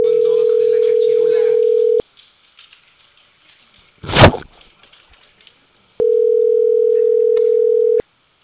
Ringback tone (North America), 440 + 480 Hz.